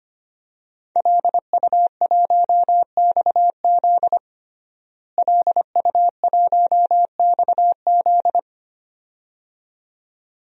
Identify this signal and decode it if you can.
Morse code, heard as a keyed tone, about 25 wpm, 700 Hz. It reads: LU1XZ LU1XZ
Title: DXCC-100 Call Signs - 5 Characters - Repeated 2x - 25wpm